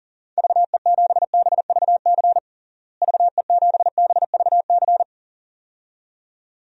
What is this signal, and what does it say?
Morse code, heard as a keyed tone, about 40 wpm, 700 Hz. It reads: VE7BVC VE7BVC
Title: DXCC-100 Call Signs - 6 Characters - Repeated 2x - 40wpm